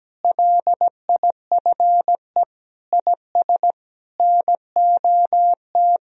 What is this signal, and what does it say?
Morse code, heard as a keyed tone, about 17 wpm, 700 Hz. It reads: LIFE IS NOT
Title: Sentences from Top 1000 Words - Review Part 4 - 17wpm